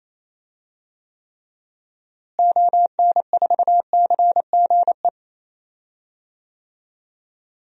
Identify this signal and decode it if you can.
Morse code, heard as a keyed tone, about 28 wpm, 700 Hz. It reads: ON4CGE
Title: DXCC-100 Call Signs - 6 Characters - Repeated 2x - 28wpm